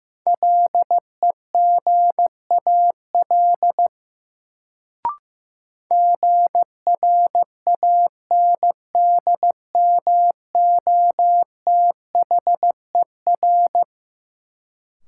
Morse code, heard as a keyed tone, about 16 wpm, 700 Hz. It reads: LEGAL GRANDMOTHER
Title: Sentences from Top 1200 Words - New Word Drill - 15wpm